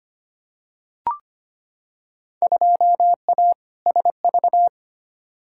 Morse code, heard as a keyed tone, about 25 wpm, 700 Hz. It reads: 2A SV
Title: Field Day Exchanges - 25wpm